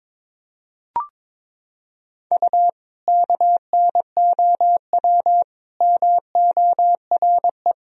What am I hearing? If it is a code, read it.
Morse code, heard as a keyed tone, about 22 wpm, 700 Hz. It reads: U KNOW MORE
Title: Sentences from Top 100 Words and Abbreviations 22wpm